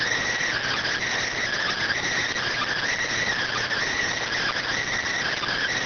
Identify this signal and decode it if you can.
Emergency Alert System (SAME) tones (header bursts), heard as harsh buzzing bursts. It is heard from 0.5 to 4.9 s.